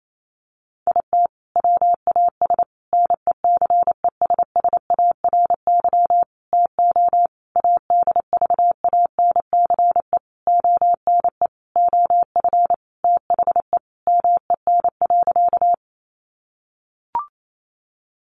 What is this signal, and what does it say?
Morse code, heard as a keyed tone, about 28 wpm, 700 Hz. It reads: IT WAS NECESSARY TO ADVANCE ONE OF THE MEN.
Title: Sentences from Top 1400 Words 28wpm